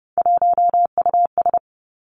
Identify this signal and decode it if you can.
Morse code, heard as a keyed tone, about 30 wpm, 700 Hz. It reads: 1US
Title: Three Challenging Letter-Number Combinations 30wpm